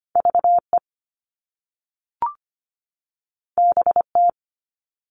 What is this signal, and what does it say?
Morse code, heard as a keyed tone, about 25 wpm, 700 Hz. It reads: VE BT
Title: All Cut Numbers - 2-Digit 25wpm